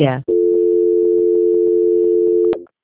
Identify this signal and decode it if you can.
Dial tone (North America), 350 + 440 Hz, with speech.
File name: out- -2004- - - .70